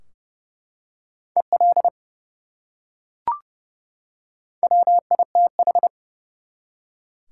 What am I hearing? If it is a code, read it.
Morse code, heard as a keyed tone, about 30 wpm, 700 Hz. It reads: EL WITH
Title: Top 180 QSO Elements 30wpm